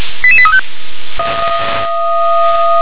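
Five-tone selective call (ZVEI), heard as quick runs of five tones, tones of about 70 ms. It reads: 88835
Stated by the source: Five-tone ZVEI selective call received by SDR.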